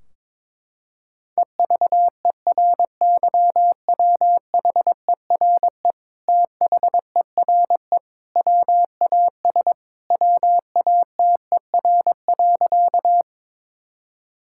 Morse code, heard as a keyed tone, about 22 wpm, 700 Hz. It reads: EVERYWHERE THERE WAS WATER.
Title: Sentences from Top 1600 Words 22wpm